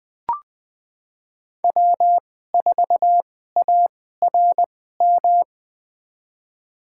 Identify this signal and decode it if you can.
Morse code, heard as a keyed tone, about 21 wpm, spaced as for 15 wpm, 700 Hz. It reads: W4ARM
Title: USA Call Signs - 5 Characters - 2x Character Spacing - 20wpm